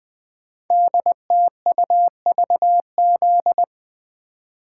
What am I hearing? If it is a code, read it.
Morse code, heard as a keyed tone, about 20 wpm, 700 Hz. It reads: DTUVZ
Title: Code Groups - 5 Letters - 20wpm